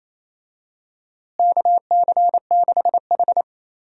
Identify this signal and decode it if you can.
Morse code, heard as a keyed tone, about 28 wpm, 700 Hz. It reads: KC6H
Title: USA Call Signs - 4 Characters - Repeated 2x - 28wpm